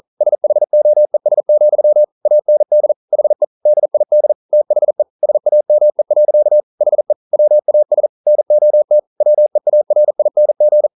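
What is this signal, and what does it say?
Morse code, heard as a keyed tone, about 42 wpm, 600 Hz. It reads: SHOES, AND HE DID THE SAME. HE WAS NOT WEARING
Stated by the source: Morse code practice at 41 wpm, from CW med Gnister.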